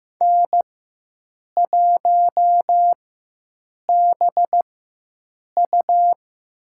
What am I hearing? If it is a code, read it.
Morse code, heard as a keyed tone, about 15 wpm, spaced as for 7 wpm, 700 Hz. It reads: N1BU
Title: USA Call Signs - 4x Character Spacing - 15wpm